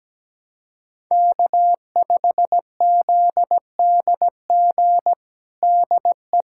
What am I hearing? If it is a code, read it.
Morse code, heard as a keyed tone, about 17 wpm, 700 Hz. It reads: K5ZDG DE
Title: USA Callsign DE Callsign - 5 Characters - 17wpm